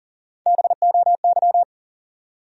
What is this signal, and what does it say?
Morse code, heard as a keyed tone, about 56 wpm, spaced as for 48 wpm, 700 Hz. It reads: BOY
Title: Speed-Racing Top 200 Words 40wpm